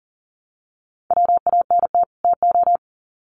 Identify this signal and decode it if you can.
Morse code, heard as a keyed tone, about 40 wpm, 700 Hz. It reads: WANT TO
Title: Sets of 2 Words 40wpm